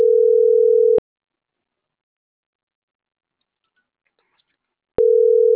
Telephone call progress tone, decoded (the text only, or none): ringback tone